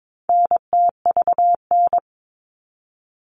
Morse code, heard as a keyed tone, about 22 wpm, 700 Hz. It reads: NTVN
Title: All Cut Numbers - 4-Digit 22wpm